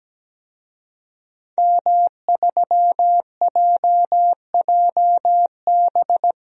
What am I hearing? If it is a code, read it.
Morse code, heard as a keyed tone, about 17 wpm, 700 Hz. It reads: M3JJB
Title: DXCC-100 Call Signs - 5 Characters - 17wpm